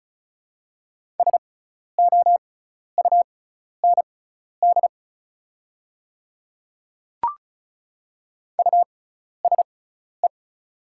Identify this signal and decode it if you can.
Morse code, heard as a keyed tone, about 36 wpm, spaced as for 12 wpm, 690 Hz. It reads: SOUND USE
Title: 6x Character Spacing - Top 100 Words - 35wpm